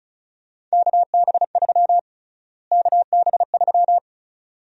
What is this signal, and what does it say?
Morse code, heard as a keyed tone, about 34 wpm, 700 Hz. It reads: KB3 KB3
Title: USA Call Sign Prefixes - Repeated 2x - 35wpm